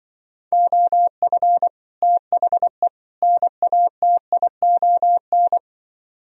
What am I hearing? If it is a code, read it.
Morse code, heard as a keyed tone, about 24 wpm, 700 Hz. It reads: OF THE NATION
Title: Speed-Racing - Sets of 3 Words 20wpm